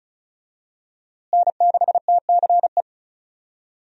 Morse code, heard as a keyed tone, about 36 wpm, 700 Hz. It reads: N6TCE
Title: USA Call Signs - 5 Characters - Repeated 3x - 35wpm